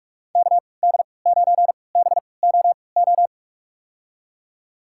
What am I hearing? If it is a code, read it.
Morse code, heard as a keyed tone, about 45 wpm, spaced as for 25 wpm, 700 Hz. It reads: KD9BOO
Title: USA Call Signs - 3x Character Spacing - 45wpm